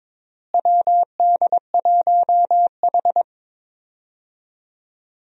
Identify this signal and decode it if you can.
Morse code, heard as a keyed tone, about 22 wpm, 700 Hz. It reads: WD1H
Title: USA Call Signs - 4 Characters - Repeated 2x - 22wpm